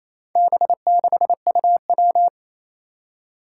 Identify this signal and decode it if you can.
Morse code, heard as a keyed tone, about 28 wpm, 700 Hz. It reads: B6UW
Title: Four Challenging Letter-Number Combinations 28wpm